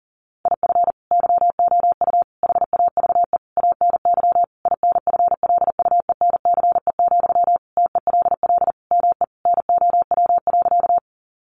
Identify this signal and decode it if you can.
Morse code, heard as a keyed tone, about 40 wpm, 700 Hz. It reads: IF YOU HAVE ANY INFLUENCE, TELL ME NOW.
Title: Sentences from Top 1100 Words 40wpm